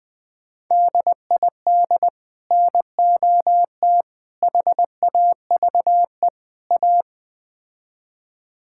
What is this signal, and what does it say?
Morse code, heard as a keyed tone, about 20 wpm, 700 Hz. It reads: DID NOT HAVE A
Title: Build A Phrase - 4-Word Phrases - 20wpm